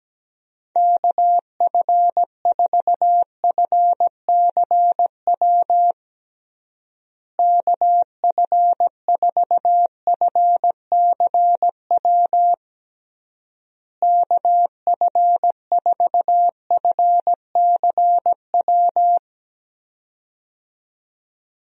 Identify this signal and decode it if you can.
Morse code, heard as a keyed tone, about 17 wpm, 700 Hz. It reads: KF4FCW KF4FCW KF4FCW
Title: USA Call Signs - 6 Characters - Repeated 3x - 17wpm